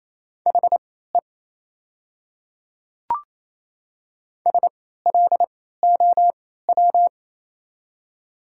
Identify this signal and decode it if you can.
Morse code, heard as a keyed tone, about 28 wpm, spaced as for 16 wpm, 700 Hz. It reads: HE SLOW
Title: 3x Character Spacing - Top 500 Words - 28wpm